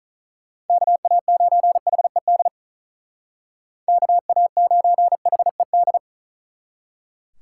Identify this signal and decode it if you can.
Morse code, heard as a keyed tone, about 41 wpm, 700 Hz. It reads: KA9HED KA9HED
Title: Speed Racing - USA 2x3 Call Signs 35wpm